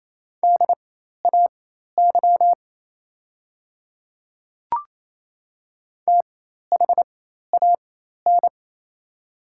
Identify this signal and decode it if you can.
Morse code, heard as a keyed tone, about 28 wpm, spaced as for 13 wpm, 700 Hz. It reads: DAY THAN
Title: 4x Character Spacing - Top 100 Words - 28wpm